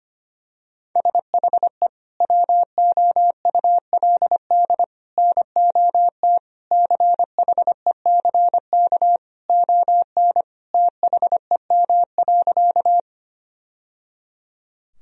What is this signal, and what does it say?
Morse code, heard as a keyed tone, about 25 wpm, 700 Hz. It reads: SHE WOULD NOT CHECK ON THEM.
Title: Sentences from Top 1000 Words - Review Part 3 - 25wpm